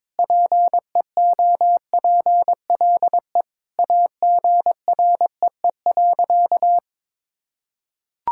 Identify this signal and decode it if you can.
Morse code, heard as a keyed tone, about 22 wpm, 700 Hz. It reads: PEOPLE AGREE.
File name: practice / 22wpm / Sentences with 3-Word Phrases - 22wpm